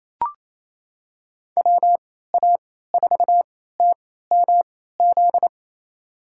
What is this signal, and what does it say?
Morse code, heard as a keyed tone, about 28 wpm, spaced as for 16 wpm, 700 Hz. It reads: WA4TMZ
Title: USA Call Signs - 3x Character Spacing - 28wpm